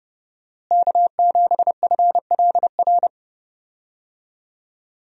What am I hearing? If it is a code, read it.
Morse code, heard as a keyed tone, about 30 wpm, 700 Hz. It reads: K7FLR
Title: USA Call Signs - 5 Characters - Repeated 3x - 30wpm